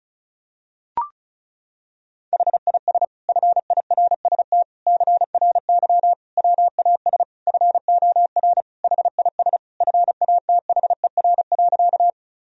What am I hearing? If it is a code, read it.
Morse code, heard as a keyed tone, about 36 wpm, 700 Hz. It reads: HIS FIRST CRY WAS FOR HIS FATHER.
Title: Sentences from Top 1000 Words - Review Part 2 - 35wpm